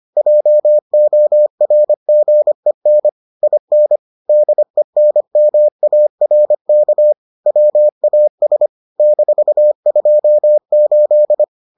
Morse code, heard as a keyed tone, about 25 wpm, 600 Hz. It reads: JORGEN IN DENMARK WAS -28